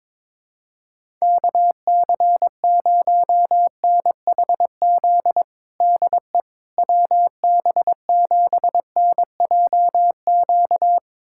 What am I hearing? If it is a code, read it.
Morse code, heard as a keyed tone, about 22 wpm, 700 Hz. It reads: KC0NHZ DE WB7NJQ
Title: USA Callsign DE Callsign - 6 Characters - 22wpm